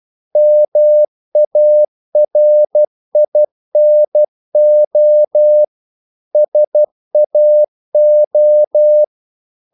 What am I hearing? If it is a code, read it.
Morse code, heard as a keyed tone, about 12 wpm, 600 Hz. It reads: MARINO SAO
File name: CW / Practice / NationalCapitals12